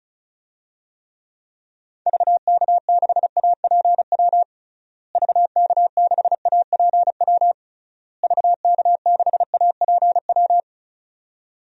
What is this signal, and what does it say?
Morse code, heard as a keyed tone, about 36 wpm, 700 Hz. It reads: VK6APW VK6APW VK6APW
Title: DXCC-100 Call Signs - 6 Characters - Repeated 3x - 35wpm